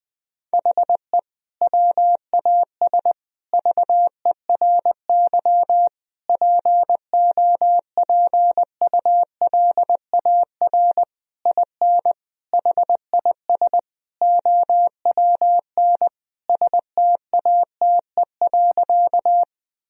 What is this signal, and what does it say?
Morse code, heard as a keyed tone, about 20 wpm, 700 Hz. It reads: HE WAS VERY POPULAR IN HIS OWN STATE.
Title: Sentences from Top 1000 Words - Review Part 3 - 20wpm